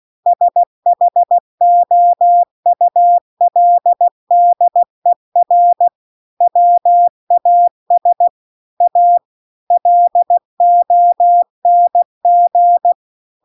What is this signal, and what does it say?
Morse code, heard as a keyed tone, about 16 wpm, 700 Hz. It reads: SHOULDER WAS A LONG
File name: CW-The-Big-Twoheaded-River-2-16WPM